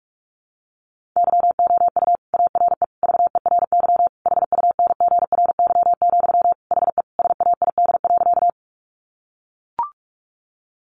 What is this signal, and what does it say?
Morse code, heard as a keyed tone, about 44 wpm, 700 Hz. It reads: YOU ARE VERY HUNGRY, HE SAID.
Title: Sentences from Top 1400 Words 45wpm